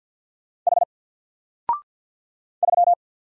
Morse code, heard as a keyed tone, about 50 wpm, 700 Hz. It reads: H 3
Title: Single Letter-Number 50wpm